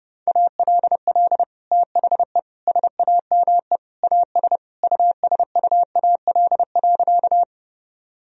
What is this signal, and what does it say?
Morse code, heard as a keyed tone, about 30 wpm, 700 Hz. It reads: ALL THE SAME AS USUAL.
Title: Sentences from Top 1000 Words - Review Part 2 - 30wpm